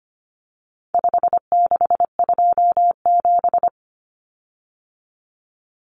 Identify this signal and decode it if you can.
Morse code, heard as a keyed tone, about 25 wpm, 700 Hz. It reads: 5627
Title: Common Cut Numbers - 4-Digit 25wpm